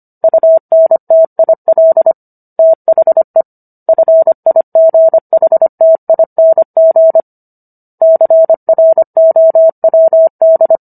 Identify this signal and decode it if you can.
Morse code, heard as a keyed tone, about 25 wpm, 650 Hz. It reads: UNTIL THE FIGHTING CROWD